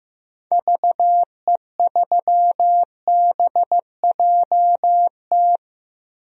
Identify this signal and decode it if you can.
Morse code, heard as a keyed tone, about 15 wpm, 700 Hz. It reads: VE3BJT
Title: DXCC-100 Call Signs - 6 Characters - Repeated 3x - 15wpm